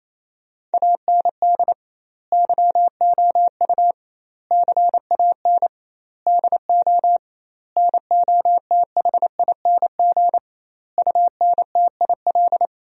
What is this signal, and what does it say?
Morse code, heard as a keyed tone, about 28 wpm, 700 Hz. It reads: AND YOU CAN DO NOTHING UNTIL
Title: 2x Word-Spacing - Sentences from Top 500 Words - 28wpm